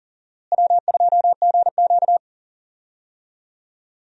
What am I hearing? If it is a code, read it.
Morse code, heard as a keyed tone, about 37 wpm, 700 Hz. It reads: W2GQ
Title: USA Call Signs - 4 Characters - Repeated 3x - 40wpm